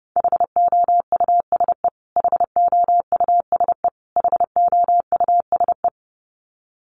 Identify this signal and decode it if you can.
Morse code, heard as a keyed tone, about 30 wpm, 700 Hz. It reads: HOUSE HOUSE HOUSE
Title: Top 200 Words Repeated 30wpm